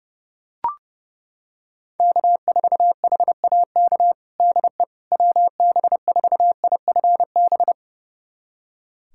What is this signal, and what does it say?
Morse code, heard as a keyed tone, about 30 wpm, 700 Hz. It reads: K4HAK DE WB4IFB
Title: USA Callsign DE Callsign 30wpm